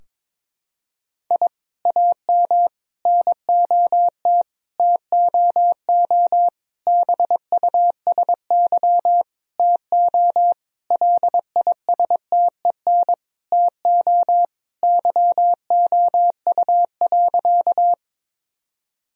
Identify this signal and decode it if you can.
Morse code, heard as a keyed tone, about 22 wpm, 700 Hz. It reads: I AM NOT TOO BUSY TO LISTEN TO YOU.
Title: Sentences from Top 1000 Words - Review Part 3 - 22wpm